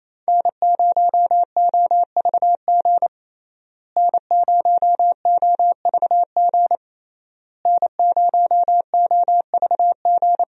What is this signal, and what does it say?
Morse code, heard as a keyed tone, about 27 wpm, 700 Hz. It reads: N0OVG N0OVG N0OVG
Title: USA Call Signs - 5 Characters - Repeated 3x - 28wpm